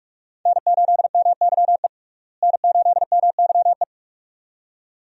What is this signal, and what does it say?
Morse code, heard as a keyed tone, about 46 wpm, 700 Hz. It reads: N8MYE N8MYE
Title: USA Call Signs - 5 Characters - Repeated 2x - 45wpm